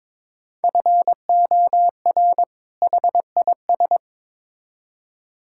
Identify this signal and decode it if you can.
Morse code, heard as a keyed tone, about 22 wpm, 700 Hz. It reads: FOR HIS
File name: Build A Phrase - 2-Word Phrases - 22wpm